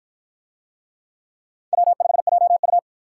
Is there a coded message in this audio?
Yes: Morse code from 1.7 to 2.8 s, heard as a keyed tone, about 53 wpm, 700 Hz. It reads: W5JU